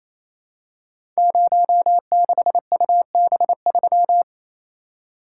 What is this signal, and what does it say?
Morse code, heard as a keyed tone, about 27 wpm, 700 Hz. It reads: 06UB3
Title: Code Groups - 5 Letter:Numbers - 28wpm